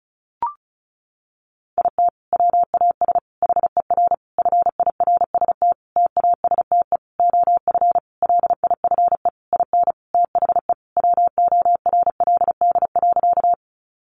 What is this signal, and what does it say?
Morse code, heard as a keyed tone, about 34 wpm, 700 Hz. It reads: IT WAS HER FIRST TASTE OF LIFE IN THE WORLD.
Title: Sentences from Top 1100 Words 35wpm